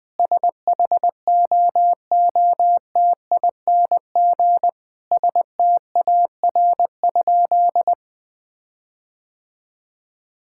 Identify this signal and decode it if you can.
Morse code, heard as a keyed tone, about 20 wpm, 700 Hz. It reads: SHOOTING STAR?
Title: Sentences with 8-Word Phrases - 20wpm